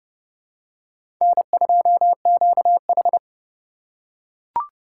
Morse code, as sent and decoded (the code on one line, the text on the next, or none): -. ..--- --.- ....
N2QH